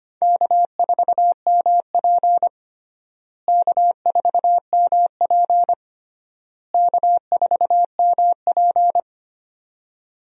Morse code, heard as a keyed tone, about 25 wpm, 700 Hz. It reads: K4MP K4MP K4MP
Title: USA Call Signs - 4 Characters - Repeated 3x - 25wpm